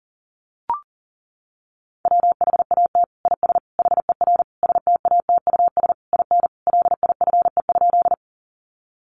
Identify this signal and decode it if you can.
Morse code, heard as a keyed tone, about 37 wpm, 700 Hz. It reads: WHAT IS HER STATUS IN LIFE?
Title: Sentences from Top 1400 Words 40wpm